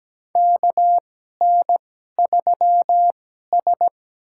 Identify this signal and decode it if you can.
Morse code, heard as a keyed tone, about 17 wpm, spaced as for 12 wpm, 700 Hz. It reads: KN3S
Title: USA Call Signs - 2x Character Spacing - 17wpm